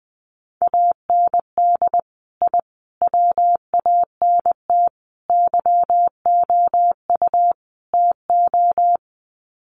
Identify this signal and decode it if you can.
Morse code, heard as a keyed tone, about 20 wpm, 700 Hz. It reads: AND I WANT YOU TO
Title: Sets of 5 Words Encore 20wpm